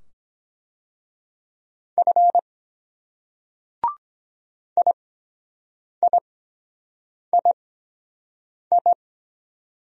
Morse code, heard as a keyed tone, about 24 wpm, spaced as for 7 wpm, 700 Hz. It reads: F IIII